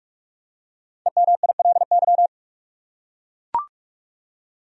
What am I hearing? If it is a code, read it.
Morse code, heard as a keyed tone, about 43 wpm, 700 Hz. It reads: EMILY
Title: USA Common Names - 45wpm